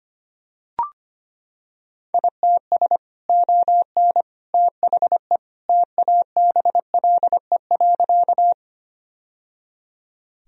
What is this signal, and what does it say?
Morse code, heard as a keyed tone, about 25 wpm, 700 Hz. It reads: ITS ON THE TABLE.
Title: Sentences from Top 1000 Words - Review Part 2 - 25wpm